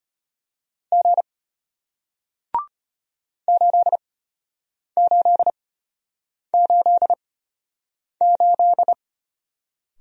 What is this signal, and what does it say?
Morse code, heard as a keyed tone, about 34 wpm, spaced as for 8 wpm, 700 Hz. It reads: G 8888